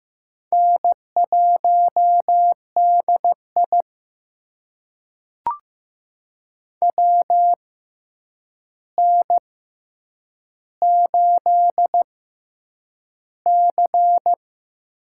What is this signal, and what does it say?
Morse code, heard as a keyed tone, about 15 wpm, spaced as for 8 wpm, 700 Hz. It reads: N1DI WN8C
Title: USA Call Signs - 6x Character Spacing - 15wpm